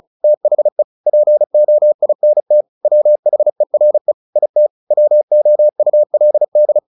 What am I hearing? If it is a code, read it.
Morse code, heard as a keyed tone, about 35 wpm, 600 Hz. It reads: THE POINT WHERE IT WOULD
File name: the old man and the sea 2 35wpm